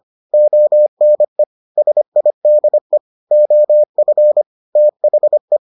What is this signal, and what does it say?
Morse code, heard as a keyed tone, about 25 wpm, 600 Hz. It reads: ONE SIDE OF THE